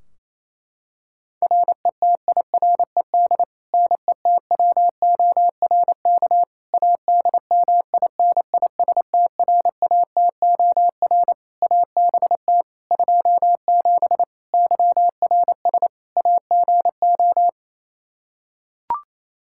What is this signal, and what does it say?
Morse code, heard as a keyed tone, about 28 wpm, 700 Hz. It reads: RETIRED NETWORK ADMINISTRATOR ABT 27 YRS AGO
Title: K1USN SST QSO Practice 28wpm